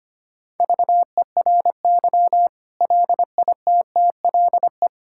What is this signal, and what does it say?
Morse code, heard as a keyed tone, about 25 wpm, 700 Hz. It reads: VERY LITTLE
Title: Sentences from Top 1000 Words - Review Part 6 - 25wpm